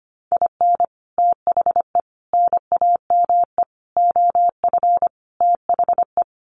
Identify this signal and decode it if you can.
Morse code, heard as a keyed tone, about 25 wpm, 700 Hz. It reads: IN THE NAME OF THE
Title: Sets of 5 Words Encore 25wpm